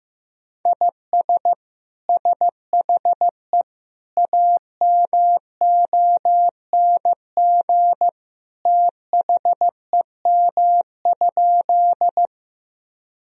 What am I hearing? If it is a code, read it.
Morse code, heard as a keyed tone, about 15 wpm, 700 Hz. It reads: IS SHE AMONG THEM?
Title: Sentences from Top 1000 Words - Review Part 3 - 15wpm